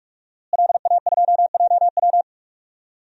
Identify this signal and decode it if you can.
Morse code, heard as a keyed tone, about 46 wpm, 700 Hz. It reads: LA2JW
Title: DXCC-100 Call Signs - 5 Characters - Repeated 2x - 45wpm